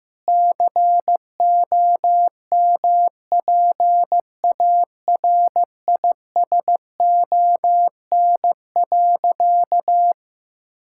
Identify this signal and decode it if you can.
Morse code, heard as a keyed tone, about 15 wpm, 700 Hz. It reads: COMPARISON.
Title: Sentences from Top 1600 Words 15wpm